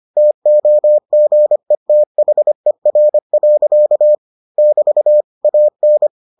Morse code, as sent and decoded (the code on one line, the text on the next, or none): - --- --. - .... .-. .-.-.- / -...- / .- -.
TOGETHER. <BT> AN